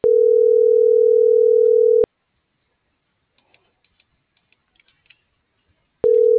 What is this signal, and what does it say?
Ringback tone (North America), 440 + 480 Hz.